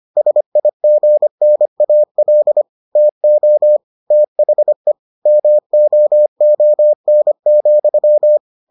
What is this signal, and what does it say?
Morse code, heard as a keyed tone, about 25 wpm, 600 Hz. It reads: SIGNAL TO THE MOON,